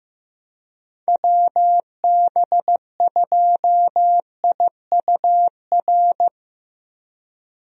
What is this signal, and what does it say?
Morse code, heard as a keyed tone, about 15 wpm, 700 Hz. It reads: WB2IUR
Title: USA Call Signs - 6 Characters - 15wpm